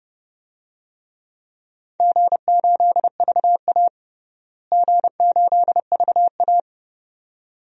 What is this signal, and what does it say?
Morse code, heard as a keyed tone, about 32 wpm, 700 Hz. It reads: G8VA G8VA
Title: DXCC-100 Call Signs - 4 Characters - Repeated 2x - 30wpm